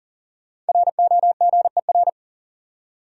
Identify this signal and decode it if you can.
Morse code, heard as a keyed tone, about 40 wpm, 700 Hz. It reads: ROGER
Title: USA Common Names - Max 5 Letters - 40wpm